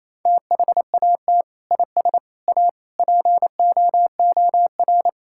Morse code, heard as a keyed tone, about 28 wpm, 700 Hz. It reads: THAT IS A POOR
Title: Sentences from Top 1000 Words - Review Part 6 - 28wpm